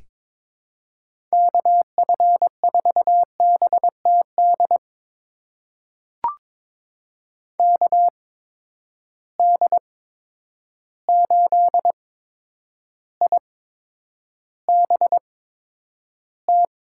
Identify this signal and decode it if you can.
Morse code, heard as a keyed tone, about 21 wpm, spaced as for 9 wpm, 700 Hz. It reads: KF4BTD KD8IBT
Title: USA Call Signs - 6 Characters - 8x Character Spacing - 22wpm